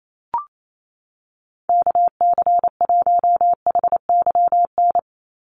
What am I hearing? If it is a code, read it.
Morse code, heard as a keyed tone, about 28 wpm, 700 Hz. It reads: KC1HYN
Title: USA Call Signs 28wpm